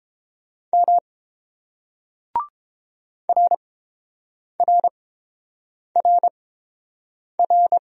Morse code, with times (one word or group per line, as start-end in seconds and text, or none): 0.7-1.0 M
3.3-7.8 RRRR